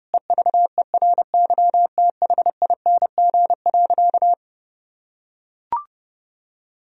Morse code, heard as a keyed tone, about 30 wpm, 700 Hz. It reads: EVERYTHING.
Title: Sentences from Top 1000 Words - Review Part 1 - 30wpm